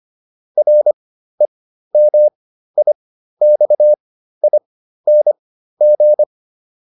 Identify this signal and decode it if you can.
Morse code, heard as a keyed tone, about 25 wpm, spaced as for 13 wpm, 600 Hz. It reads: REMIXING
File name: RE8 13